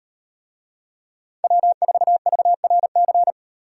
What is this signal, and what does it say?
Morse code, heard as a keyed tone, about 38 wpm, 700 Hz. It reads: W4VRC